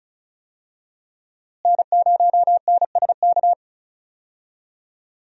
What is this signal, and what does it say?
Morse code, heard as a keyed tone, about 34 wpm, 700 Hz. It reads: N0NSK
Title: USA Call Signs - 5 Characters - Repeated 2x - 35wpm